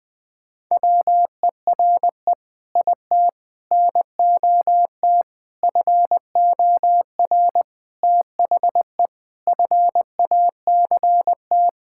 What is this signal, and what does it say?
Morse code, heard as a keyed tone, about 21 wpm, 700 Hz. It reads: WERE IT NOT FOR THE FACT
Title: Sentences with 7-Word Phrases - Spoken Beginning - 20wpm